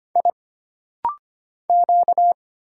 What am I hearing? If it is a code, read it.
Morse code, heard as a keyed tone, about 25 wpm, 700 Hz. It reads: I Q
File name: Single Character (Letter:Number:Punctuation:Prosign) - Rapid-Fire - 25wpm